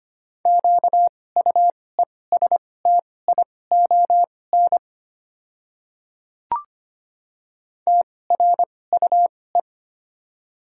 Morse code, heard as a keyed tone, about 25 wpm, spaced as for 18 wpm, 700 Hz. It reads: QUESTION TRUE
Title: 2x Character Spacing - Top 500 Words - 25wpm